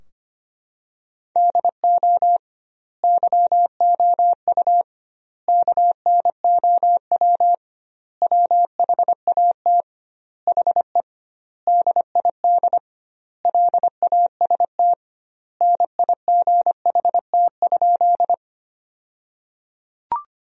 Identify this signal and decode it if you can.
Morse code, heard as a keyed tone, about 25 wpm, 700 Hz. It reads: DO YOU KNOW WHAT HE DID LAST NIGHT?
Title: 2x Word-Spacing - Sentences from Top 300 Words - 25wpm